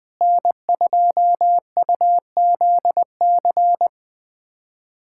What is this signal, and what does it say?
Morse code, heard as a keyed tone, about 21 wpm, 700 Hz. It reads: N2UZC
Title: USA Callsign DE Callsign - 5 Characters - 20wpm